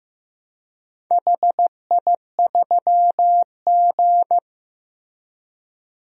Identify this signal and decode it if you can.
Morse code, heard as a keyed tone, about 15 wpm, 700 Hz. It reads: HI3G
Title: DXCC-100 Call Signs - 4 Characters - Repeated 2x - 15wpm